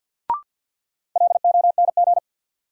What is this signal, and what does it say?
Morse code, heard as a keyed tone, about 53 wpm, 700 Hz. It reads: LONG
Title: Sets of 2 Words - New Word Drill - 50wpm